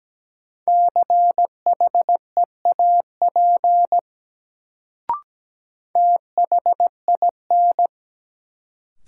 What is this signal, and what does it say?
Morse code, heard as a keyed tone, about 17 wpm, 700 Hz. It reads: CHEAP THIN
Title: Sentences from Top 1200 Words - New Word Drill - 17wpm